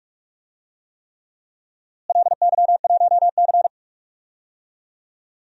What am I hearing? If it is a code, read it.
Morse code, heard as a keyed tone, about 41 wpm, 700 Hz. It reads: LY1C
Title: DXCC-100 Call Signs - 4 Characters - Repeated 3x - 45wpm